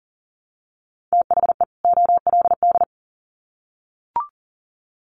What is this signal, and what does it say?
Morse code, heard as a keyed tone, about 40 wpm, 700 Hz. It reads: THE OLD
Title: Sets of 2 Words 40wpm